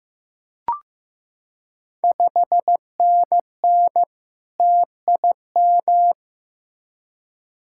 Morse code, heard as a keyed tone, about 15 wpm, 700 Hz. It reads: 5NN TIM
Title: Ontario QSO Party Exchange 15wpm